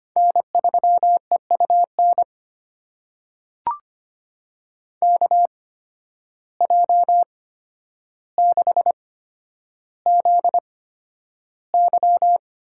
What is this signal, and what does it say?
Morse code, heard as a keyed tone, about 25 wpm, spaced as for 11 wpm, 700 Hz. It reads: N3EUN KJ6ZY
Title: USA Call Signs - 5 Characters - 8x Character Spacing - 25wpm